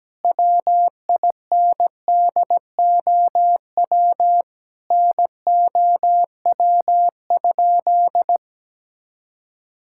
Morse code, heard as a keyed tone, about 17 wpm, 700 Hz. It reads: WINDOW NOW?
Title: Sentences from Top 1600 Words 17wpm